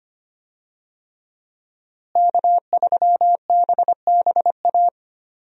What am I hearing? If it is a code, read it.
Morse code, heard as a keyed tone, about 25 wpm, 700 Hz. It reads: K3BBA
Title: USA Call Signs - 5 Characters - Repeated 3x - 25wpm